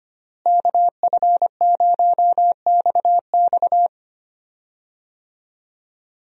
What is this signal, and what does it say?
Morse code, heard as a keyed tone, about 25 wpm, 700 Hz. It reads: KF0XX
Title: USA Call Signs - 5 Characters - Repeated 2x - 25wpm